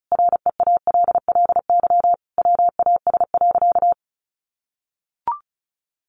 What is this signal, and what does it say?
Morse code, heard as a keyed tone, about 34 wpm, 700 Hz. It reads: REALLY WAS.
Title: Sentences from Top 600 Words 35wpm